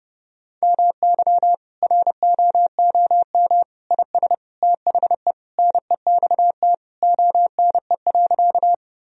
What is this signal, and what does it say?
Morse code, heard as a keyed tone, about 30 wpm, 700 Hz. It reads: MY ROOM IS THE NEXT ONE.
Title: Sentences from Top 1000 Words - Review Part 1 - 30wpm